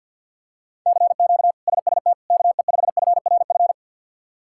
Morse code, heard as a keyed tone, about 52 wpm, 700 Hz. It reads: CQ SST KE5FRF
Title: K1USN SST QSO Practice - 50wpm